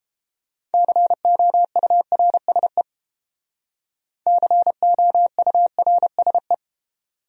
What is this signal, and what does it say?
Morse code, heard as a keyed tone, about 29 wpm, 700 Hz. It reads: COURSE COURSE